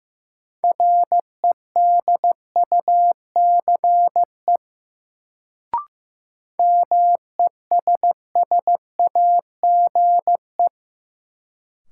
Morse code, heard as a keyed tone, about 15 wpm, 700 Hz. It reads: REDUCE MESSAGE
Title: Sentences from Top 1100 Words - New Word Drill Alt - 15wpm